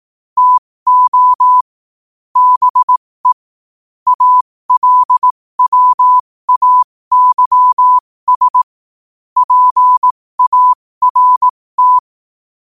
Morse code, heard as a keyed tone, about 18 wpm, spaced as for 15 wpm, 1000 Hz. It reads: TO BE ALWAYS PART
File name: qotdfiles / 15wpm